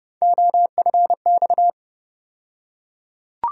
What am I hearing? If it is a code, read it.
Morse code, heard as a keyed tone, about 30 wpm, 700 Hz. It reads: OFX
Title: USA Call Sign Postfixes - 3 Letters 30wpm